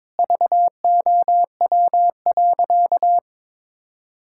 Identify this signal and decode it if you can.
Morse code, heard as a keyed tone, about 22 wpm, 700 Hz. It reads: VOW.
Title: Pangram Challenge v2 - 22wpm